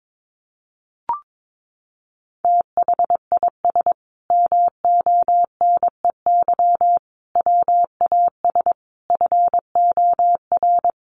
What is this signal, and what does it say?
Morse code, heard as a keyed tone, about 22 wpm, 700 Hz. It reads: THIS MONEY WAS FOR
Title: Sentences from Top 400 Words 22wpm